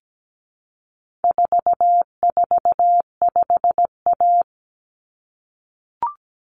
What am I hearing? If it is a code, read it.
Morse code, heard as a keyed tone, about 17 wpm, 700 Hz. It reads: 445A
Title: Common Cut Numbers - 4-Digit 17wpm